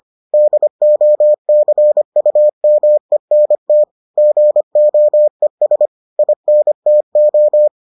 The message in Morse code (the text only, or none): DOCUMENT GOES INTO